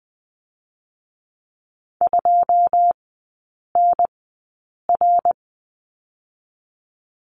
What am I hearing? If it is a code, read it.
Morse code, heard as a keyed tone, about 20 wpm, spaced as for 8 wpm, 700 Hz. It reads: 2NR
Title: 121 Lesson 20e - Number 3 - Three Characters 20wpm